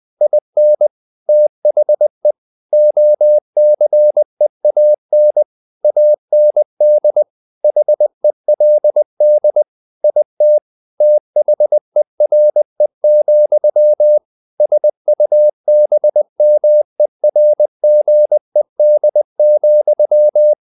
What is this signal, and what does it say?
Morse code, heard as a keyed tone, about 20 wpm, 600 Hz. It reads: IN THE OCEAN AND HELD IT THERE, SUBMERGED,
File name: the old man and the sea / the old man and the sea part 1 20wpm